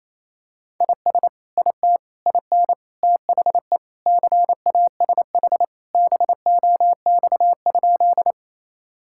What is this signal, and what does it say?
Morse code, heard as a keyed tone, about 28 wpm, 700 Hz. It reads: IS IT IN THE CASH BOX?
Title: Sentences from Top 1000 Words - Review Part 5 - 28wpm